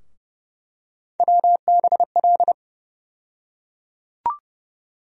Morse code, heard as a keyed tone, about 30 wpm, 700 Hz. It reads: WBL
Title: USA Call Sign Postfixes - 3 Letters 30wpm